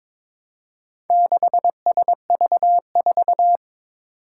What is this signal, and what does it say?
Morse code, heard as a keyed tone, about 22 wpm, 700 Hz. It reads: 6SV4
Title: Four Challenging Letter-Number Combinations 22wpm